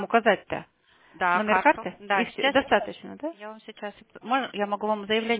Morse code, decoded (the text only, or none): none